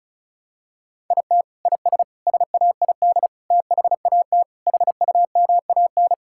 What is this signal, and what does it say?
Morse code, heard as a keyed tone, about 36 wpm, 700 Hz. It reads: IT IS SAID THAT HUMAN
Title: Sentences from Top 1000 Words - Review Part 6 - 35wpm